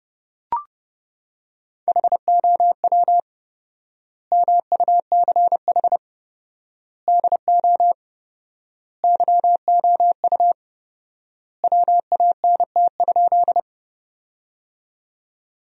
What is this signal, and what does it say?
Morse code, heard as a keyed tone, about 30 wpm, 700 Hz. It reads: HOW MUCH DO YOU WANT?
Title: 4x Word-Spacing - Sentences from Top 200 Words - 30wpm